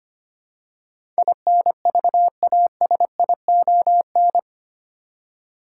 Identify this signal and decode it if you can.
Morse code, heard as a keyed tone, about 25 wpm, 700 Hz. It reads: INVASION
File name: 8 Letter Words - First 3 Letters Given - 25wpm